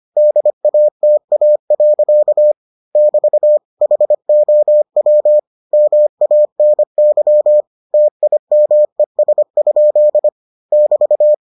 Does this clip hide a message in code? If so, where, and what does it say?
Yes: Morse code from 0.2 to 11.3 s, heard as a keyed tone, about 25 wpm, 600 Hz. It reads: DATA. <BT> HOW MANY TIMES? <BT>